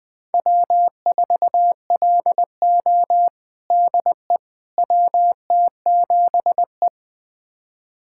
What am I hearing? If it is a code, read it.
Morse code, heard as a keyed tone, about 20 wpm, 700 Hz. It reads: W4LO DE WT7E
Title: USA Callsign DE Callsign - 4 Characters - 20wpm